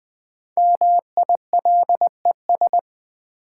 Morse code, heard as a keyed tone, about 20 wpm, 700 Hz. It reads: MILES
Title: Sentences from Top 1000 Words - Review Part 3 - New Word Drill - 20wpm